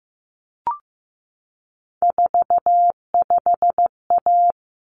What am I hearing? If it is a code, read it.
Morse code, heard as a keyed tone, about 15 wpm, 700 Hz. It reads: 45A
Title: Common Cut Numbers - 3-Digit 15wpm